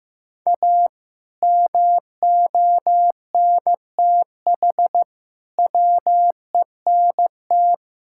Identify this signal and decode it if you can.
Morse code, heard as a keyed tone, about 15 wpm, 700 Hz. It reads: A MONTH WENT
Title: Sentences from Top 1000 Words - Review Part 5 - 15wpm